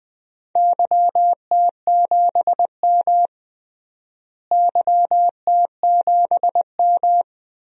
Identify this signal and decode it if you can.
Morse code, heard as a keyed tone, about 20 wpm, 700 Hz. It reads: YT7M YT7M
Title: DXCC-100 Call Signs - 4 Characters - Repeated 3x - 20wpm